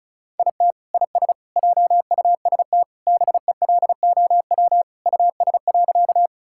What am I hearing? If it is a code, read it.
Morse code, heard as a keyed tone, about 36 wpm, 700 Hz. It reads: IT IS JUST BELOW US.
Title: Sentences from Top 1000 Words - Review Part 5 - 35wpm